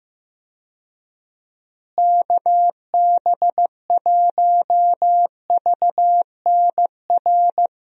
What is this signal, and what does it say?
Morse code, heard as a keyed tone, about 15 wpm, 700 Hz. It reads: KB1VNR
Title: USA Call Signs - 6 Characters - Repeated 3x - 15wpm